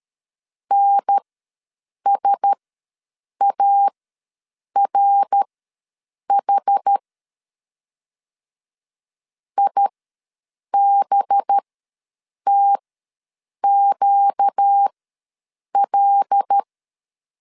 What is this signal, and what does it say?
Morse code, heard as a keyed tone, about 13 wpm, spaced as for 7 wpm, 800 Hz. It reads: NSARH IBTQL
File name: 5 LB ws 12wpm